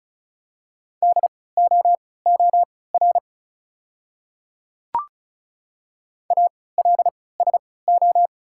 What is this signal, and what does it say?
Morse code, heard as a keyed tone, about 34 wpm, spaced as for 20 wpm, 700 Hz. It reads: DOOR ALSO
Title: 3x Character Spacing - Top 300 Words - 35wpm